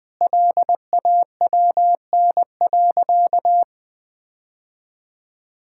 Morse code, heard as a keyed tone, about 19 wpm, 700 Hz. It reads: LAWN.
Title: Easy Sentences with Hints - 20wpm